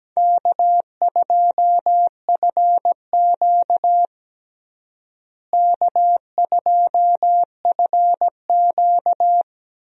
Morse code, heard as a keyed tone, about 17 wpm, 700 Hz. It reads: K2FQ K2FQ
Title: USA Call Signs - 4 Characters - Repeated 3x - 17wpm